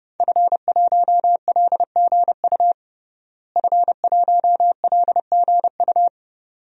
Morse code, heard as a keyed tone, about 32 wpm, 700 Hz. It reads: F1LGU F1LGU
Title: DXCC-100 Call Signs - 5 Characters - Repeated 2x - 30wpm